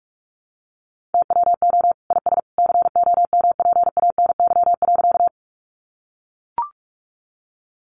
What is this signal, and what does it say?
Morse code, heard as a keyed tone, about 44 wpm, 700 Hz. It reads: TWO IS COMPANY.
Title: Sentences from Top 600 Words 45wpm